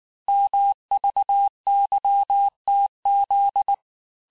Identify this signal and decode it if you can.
Morse code, heard as a keyed tone, about 19 wpm, 800 Hz. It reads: MVYTZ
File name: cours-du-mercredi / audios / cours-869-texte-CHIFFRE-vitesse-18-MPM